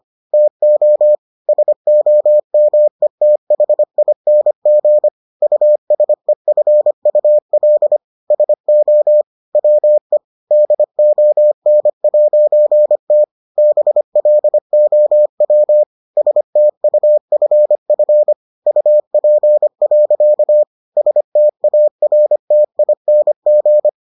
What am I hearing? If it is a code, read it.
Morse code, heard as a keyed tone, about 25 wpm, 600 Hz. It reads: TO SOMETHING USEFUL SO WE DON'T BLOW STUFF UP. STARTING